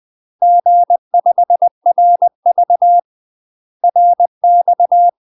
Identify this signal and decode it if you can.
Morse code, heard as a keyed tone, about 20 wpm, 700 Hz. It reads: G5RV RX
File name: UK MASTER QSOs TRAINING / 20WPM / UK MASTER QSOs 0053